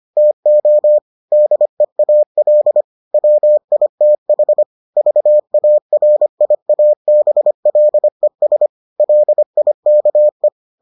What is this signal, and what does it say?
Morse code, heard as a keyed tone, about 25 wpm, 600 Hz. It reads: TO DEAL WITH VARIABLES LIKE